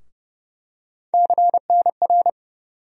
Morse code, heard as a keyed tone, about 30 wpm, 700 Hz. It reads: CNR
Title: USA Call Sign Postfixes - 3 Letters 30wpm